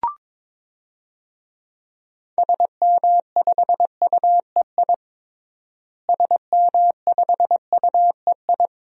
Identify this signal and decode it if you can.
Morse code, heard as a keyed tone, about 22 wpm, 700 Hz. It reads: SM5UEI SM5UEI
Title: DXCC-100 Call Signs - 6 Characters - Repeated 2x - 22wpm